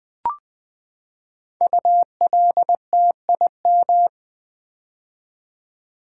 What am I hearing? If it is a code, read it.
Morse code, heard as a keyed tone, about 20 wpm, 700 Hz. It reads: ULTIM
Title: Build A Word - 5 Letters At A Time - 10-Letter Words - 20wpm